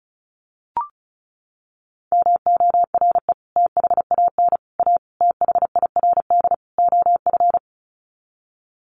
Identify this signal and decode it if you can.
Morse code, heard as a keyed tone, about 34 wpm, 700 Hz. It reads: MORE THAN A THIRD OF
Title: Sets of 5 Words Encore2 35wpm